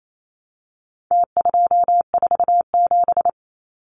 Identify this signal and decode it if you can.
Morse code, heard as a keyed tone, about 28 wpm, 700 Hz. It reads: T247
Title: Numbers - Contest Style - 4-Digit 28wpm